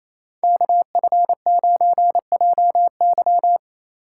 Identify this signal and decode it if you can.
Morse code, heard as a keyed tone, about 33 wpm, 700 Hz. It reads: KF9JY
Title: Speed Racing - USA 2x2 Call Signs 28wpm